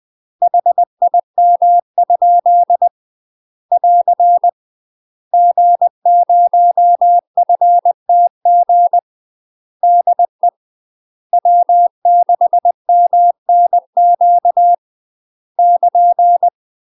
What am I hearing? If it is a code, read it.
Morse code, heard as a keyed tone, about 20 wpm, 700 Hz. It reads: HIM? <AR> G0FTG DE W6MNQ <KN>
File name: UK MASTER QSOs TRAINING/20WPM